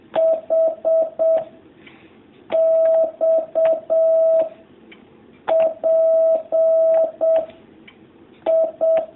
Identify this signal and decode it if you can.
Morse code, heard as a keyed tone, about 7 wpm, 650 Hz. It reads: H X P I
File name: jh8cbh / 2019cw